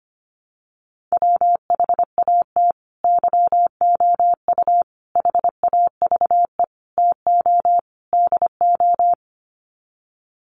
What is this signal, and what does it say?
Morse code, heard as a keyed tone, about 25 wpm, 690 Hz. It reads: WHAT YOU HAVE TO DO
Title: Sets of 5 Words Encore 25wpm